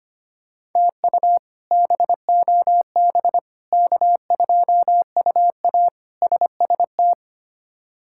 Morse code, heard as a keyed tone, about 25 wpm, 700 Hz. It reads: TU BOB K2UA SST
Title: K1USN SST QSO Practice - 25wpm